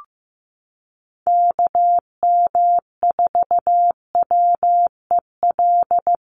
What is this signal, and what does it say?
Morse code, heard as a keyed tone, about 15 wpm, 700 Hz. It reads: KM4WEL
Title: USA Call Signs Repeated 2x 15wpm